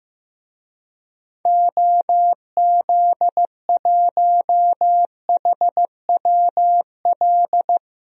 Morse code, heard as a keyed tone, about 15 wpm, 700 Hz. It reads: OZ1HWL
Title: DXCC-100 Call Signs - 6 Characters - Repeated 3x - 15wpm